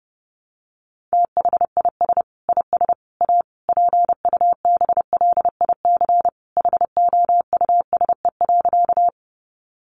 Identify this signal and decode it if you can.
Morse code, heard as a keyed tone, about 30 wpm, 700 Hz. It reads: THIS IS A PUBLIC HOUSE.
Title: Sentences from Top 600 Words 30wpm